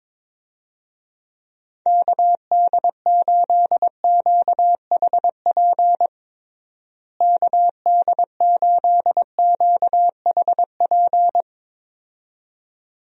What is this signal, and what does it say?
Morse code, heard as a keyed tone, about 23 wpm, 700 Hz. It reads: KD8QHP KD8QHP
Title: USA Call Signs - 6 Characters - Repeated 2x - 22wpm